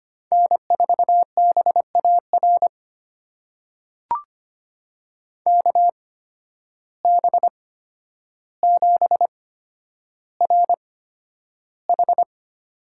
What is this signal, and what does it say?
Morse code, heard as a keyed tone, about 25 wpm, spaced as for 11 wpm, 700 Hz. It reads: N4BAR KB7RH
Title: USA Call Signs - 8x Character Spacing - 25wpm